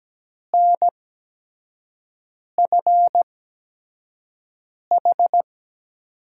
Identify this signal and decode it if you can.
Morse code, heard as a keyed tone, about 17 wpm, spaced as for 5 wpm, 700 Hz. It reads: NFH
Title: USA Call Signs - 5 Characters - 8x Character Spacing - 17wpm